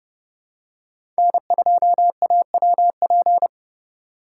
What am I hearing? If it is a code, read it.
Morse code, heard as a keyed tone, about 32 wpm, 700 Hz. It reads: N2AWP
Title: USA Call Signs - 5 Characters - 8x Character Spacing - 30wpm